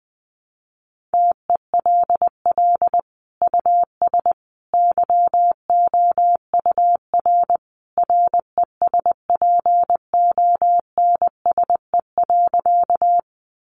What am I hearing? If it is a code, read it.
Morse code, heard as a keyed tone, about 20 wpm, 700 Hz. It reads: TELL US YOUR RESPONSE.
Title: Sentences from Top 1400 Words 20wpm